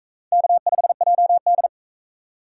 Morse code, heard as a keyed tone, about 41 wpm, 700 Hz. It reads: K5JD